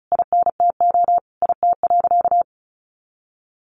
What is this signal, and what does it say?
Morse code, heard as a keyed tone, about 34 wpm, 700 Hz. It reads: INTO IT.
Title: Sentences from Top 1100 Words 35wpm